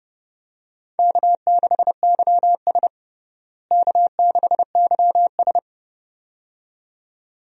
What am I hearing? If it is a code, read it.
Morse code, heard as a keyed tone, about 30 wpm, 700 Hz. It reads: K6YS K6YS
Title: USA Call Signs - 4 Characters - Repeated 2x - 30wpm